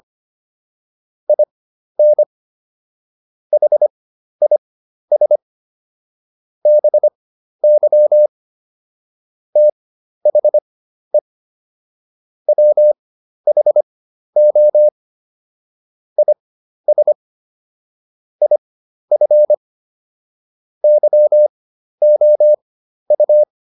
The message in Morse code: IN HIS BY THE WHO IS IF YOU

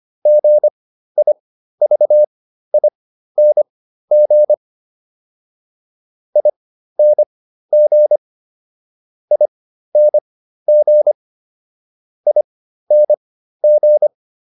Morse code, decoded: GIVING ING ING ING

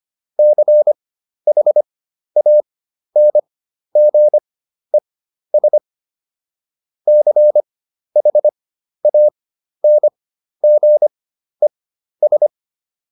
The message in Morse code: CHANGES CHANGES

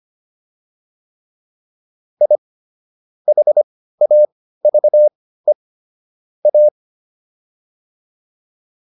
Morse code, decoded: I HAVE A

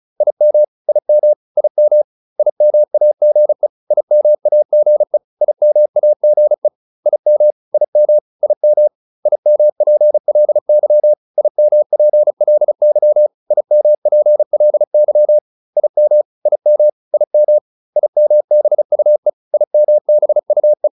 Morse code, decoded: IM IM IM IMAGE IMAGE IMAGE IM IM IM IMPLY IMPLY IMPLY IM IM IM IMBUE IMBUE

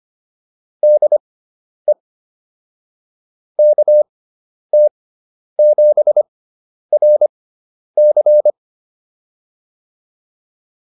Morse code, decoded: DE KT7RC

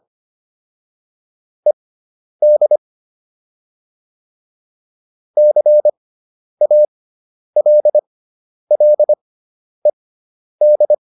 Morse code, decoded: ED CALLED

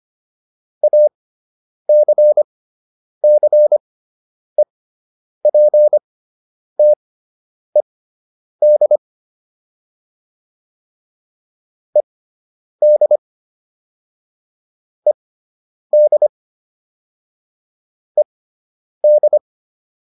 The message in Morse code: ACCEPTED ED ED ED

